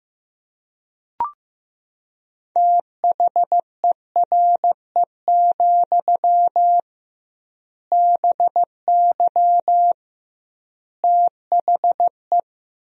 Morse code, heard as a keyed tone, about 16 wpm, 700 Hz. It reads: THERE, BY THE
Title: 2x Word-Spacing - Sentences from Top 500 Words - 15wpm